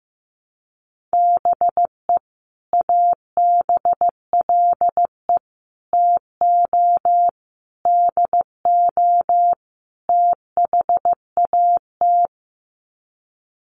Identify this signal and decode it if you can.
Morse code, heard as a keyed tone, about 14 wpm, 700 Hz. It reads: BE ABLE TO DO THAT
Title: Sets of 5 Words Encore2 15wpm